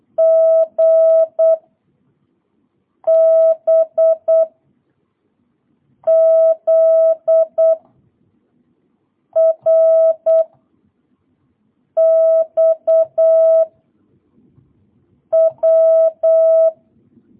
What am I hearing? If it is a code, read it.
Morse code, heard as a keyed tone, about 8 wpm, 660 Hz. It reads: G B Z R X W